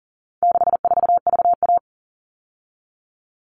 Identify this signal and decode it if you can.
Morse code, heard as a keyed tone, about 40 wpm, 700 Hz. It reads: 64VA
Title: All Cut Numbers - 4-Digit 40wpm